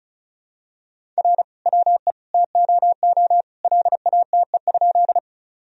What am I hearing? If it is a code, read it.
Morse code, heard as a keyed tone, about 36 wpm, 700 Hz. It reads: R WE TOO LATE?
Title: Sentences from Top 300 Words and Abbreviations 35wpm